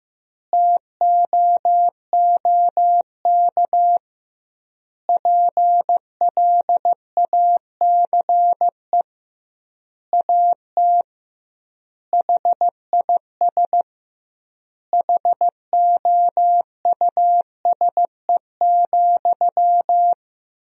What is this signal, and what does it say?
Morse code, heard as a keyed tone, about 15 wpm, 700 Hz. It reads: TOOK PLACE AT HIS HOUSE,
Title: 2x Word-Spacing - Sentences from Top 300 Words - 15wpm